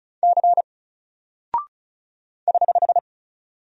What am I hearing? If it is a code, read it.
Morse code, heard as a keyed tone, about 35 wpm, 700 Hz. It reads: C <HH>
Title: Single Character (Letter:Number:Punctuation:Prosign) 35wpm